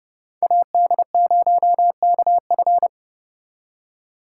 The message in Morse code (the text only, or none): AD0KF